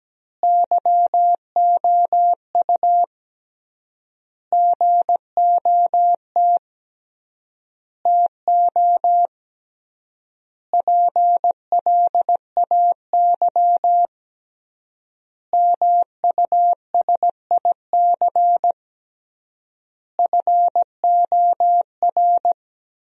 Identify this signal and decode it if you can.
Morse code, heard as a keyed tone, about 17 wpm, 700 Hz. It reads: YOU GOT TO PLAY MUSIC FOR
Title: 3x Word-Spacing - Sentences from Top 300 Words - 17wpm